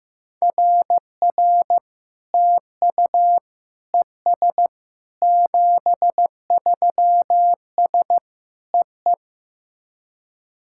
Morse code, heard as a keyed tone, about 15 wpm, 700 Hz. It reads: RR TU ES 73S EE
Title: SOTA QSO Practice 15wpm